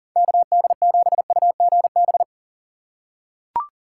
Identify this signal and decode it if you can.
Morse code, heard as a keyed tone, about 40 wpm, 700 Hz. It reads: KD7UGB
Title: USA Call Signs - 6 Characters - 2x Character Spacing - 40wpm